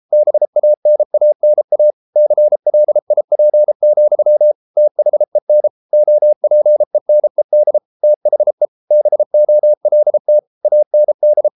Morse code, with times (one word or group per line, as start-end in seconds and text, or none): 0.1-1.9 BANANA
2.2-4.5 CLIP,
4.8-5.7 THEN
5.9-7.8 OPENED
8.0-8.7 THE
8.9-10.4 BOLT
10.6-11.5 AND